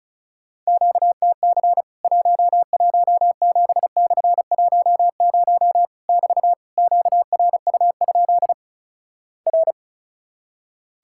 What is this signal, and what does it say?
Morse code, heard as a keyed tone, about 36 wpm, 700 Hz. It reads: QTC 117/10 <BT> QRU? R
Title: Worked All Europe DX Contest - QTC Exchange - 35wpm